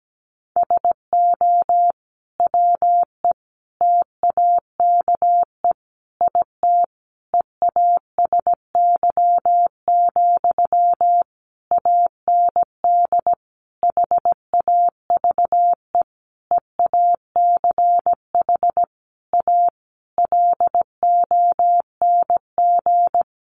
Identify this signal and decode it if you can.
Morse code, heard as a keyed tone, about 17 wpm, 700 Hz. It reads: SO WE TAKE IT EASY, AND HAVE EACH A LONG
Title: Sentences from Top 700 Words 17wpm